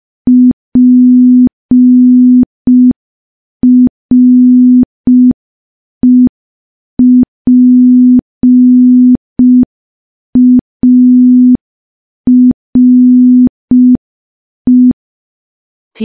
Morse code, heard as a keyed tone, about 5 wpm, 250 Hz. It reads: PREPARE
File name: morse / morse-training-05WPM